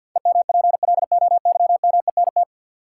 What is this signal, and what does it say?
Morse code, heard as a keyed tone, about 47 wpm, 700 Hz. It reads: EMPLOYMENT
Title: Sentences from Top 1500 Words - New Word Drill - 50wpm